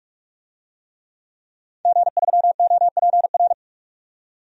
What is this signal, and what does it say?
Morse code, heard as a keyed tone, about 43 wpm, 700 Hz. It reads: G3OPR
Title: DXCC-100 Call Signs - 5 Characters - 45wpm